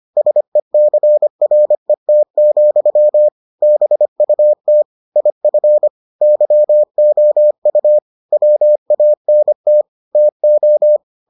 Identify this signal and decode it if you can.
Morse code, heard as a keyed tone, about 25 wpm, 600 Hz. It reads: SECRET, BUT IF YOU WANT TO